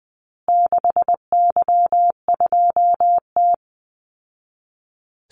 Morse code, heard as a keyed tone, about 20 wpm, 700 Hz. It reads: 6Y2T
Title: Call Signs Repeated 3x - 20wpm